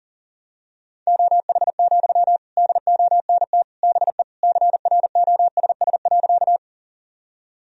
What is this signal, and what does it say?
Morse code, heard as a keyed tone, about 40 wpm, 700 Hz. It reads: OH, DONT BE CROSS.
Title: Sentences from Top 1000 Words - Review Part 1 - 40wpm